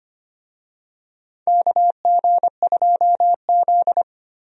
Morse code, heard as a keyed tone, about 25 wpm, 700 Hz. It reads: KG2Z
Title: USA Call Signs - 4 Characters - 25wpm